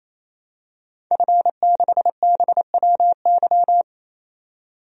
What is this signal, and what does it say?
Morse code, heard as a keyed tone, about 27 wpm, 700 Hz. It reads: F6BWY
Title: DXCC-100 Call Signs - 5 Characters - 28wpm